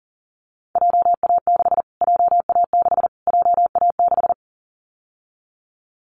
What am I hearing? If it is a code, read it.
Morse code, heard as a keyed tone, about 40 wpm, 700 Hz. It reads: JA6 JA6 JA6
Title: Callsign Prefixes for Top 25 DXCC - Repeated 3x - 40wpm